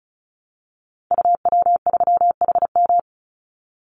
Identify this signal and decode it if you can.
Morse code, heard as a keyed tone, about 36 wpm, 700 Hz. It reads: UW3HM
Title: Call Signs - 35wpm